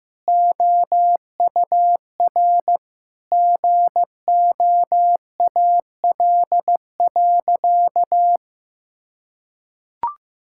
Morse code, heard as a keyed tone, about 15 wpm, 700 Hz. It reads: OUR GOAL.
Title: Sentences from Top 1000 Words - Review Part 5 - 15wpm